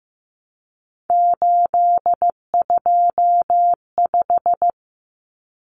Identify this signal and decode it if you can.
Morse code, heard as a keyed tone, about 15 wpm, 700 Hz. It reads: 825
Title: Common Cut Numbers - 3-Digit 15wpm